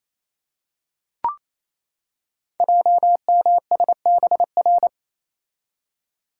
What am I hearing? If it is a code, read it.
Morse code, heard as a keyed tone, about 27 wpm, 700 Hz. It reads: JMSBR
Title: Code Groups - 5 Letters - 28wpm